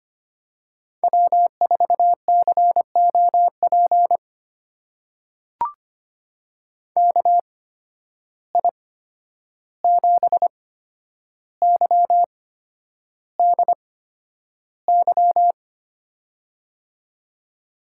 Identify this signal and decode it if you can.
Morse code, heard as a keyed tone, about 25 wpm, spaced as for 10 wpm, 700 Hz. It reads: W4COP KI7YDY
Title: USA Call Signs - 8x Character Spacing - 25wpm